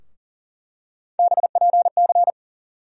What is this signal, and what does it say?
Morse code, heard as a keyed tone, about 38 wpm, 700 Hz. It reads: BPC
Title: USA Call Sign Postfixes - 3 Letters 40wpm